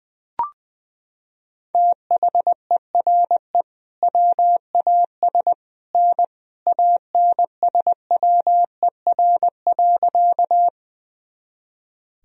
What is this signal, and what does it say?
Morse code, heard as a keyed tone, about 20 wpm, 700 Hz. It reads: THERE WAS N ANSWER.
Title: Sentences from Top 300 Words and Abbreviations 20wpm